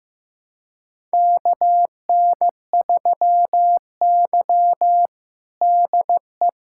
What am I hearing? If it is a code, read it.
Morse code, heard as a keyed tone, about 15 wpm, 700 Hz. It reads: KN3Y DE
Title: USA Callsign DE Callsign - 4 Characters - 15wpm